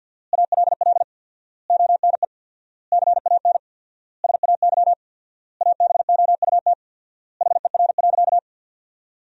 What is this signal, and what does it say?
Morse code, heard as a keyed tone, about 50 wpm, 700 Hz. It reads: ALL ONE CAN SAY ABOUT HER.
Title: 4x Word-Spacing - Sentences from Top 200 Words - 50wpm